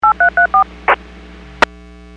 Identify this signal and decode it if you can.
DTMF, heard as telephone touch tones, tones of about 110 ms. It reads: 8334
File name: WPS / 00443 18-27 WPS 451.25 WPS